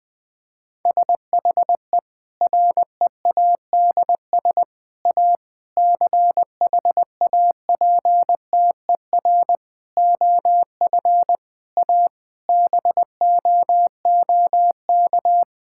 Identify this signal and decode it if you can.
Morse code, heard as a keyed tone, about 20 wpm, 700 Hz. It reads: SHE READS A CHAPTER OF A BOOK
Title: Difficult Sentences with Hints - 20wpm